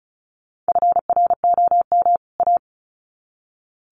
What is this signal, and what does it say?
Morse code, heard as a keyed tone, about 34 wpm, 700 Hz. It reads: FROM A
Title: Sets of 2 Words 35wpm